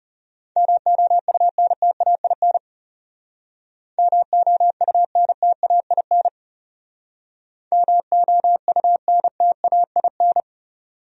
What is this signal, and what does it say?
Morse code, heard as a keyed tone, about 37 wpm, 700 Hz. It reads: MOUNTAIN MOUNTAIN MOUNTAIN
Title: Speed-Racing Top 300 Words 30wpm